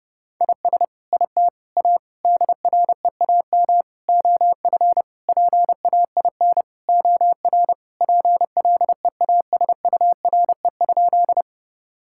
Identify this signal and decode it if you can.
Morse code, heard as a keyed tone, about 32 wpm, 700 Hz. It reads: IS IT A DREAM OF PAIN OR PLEASURE?
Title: Sentences from Top 1600 Words 30wpm